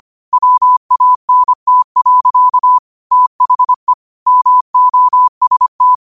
Morse code, heard as a keyed tone, about 25 wpm, 1000 Hz. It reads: WANT. THE MOST
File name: qotdfiles / 25wpm